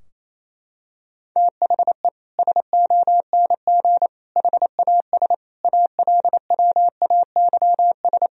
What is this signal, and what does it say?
Morse code, heard as a keyed tone, about 28 wpm, 700 Hz. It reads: THE SONG HAS ALWAYS
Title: Sentences from Top 1000 Words - Review Part 3 - 28wpm